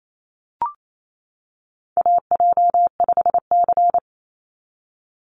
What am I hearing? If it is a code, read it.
Morse code, heard as a keyed tone, about 28 wpm, 700 Hz. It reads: AJ5C
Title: NA SOTA Chaser Callsigns 28wpm